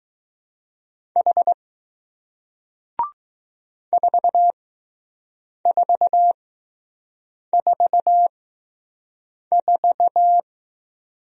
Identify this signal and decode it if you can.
Morse code, heard as a keyed tone, about 20 wpm, spaced as for 6 wpm, 700 Hz. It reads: H 4444